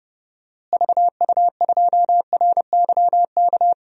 Morse code, heard as a keyed tone, about 30 wpm, 700 Hz. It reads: VU2RYK
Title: DXCC-100 Call Signs - 6 Characters - Repeated 3x - 30wpm